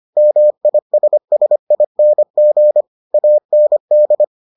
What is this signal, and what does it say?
Morse code, heard as a keyed tone, about 25 wpm, 600 Hz. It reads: MISSING AND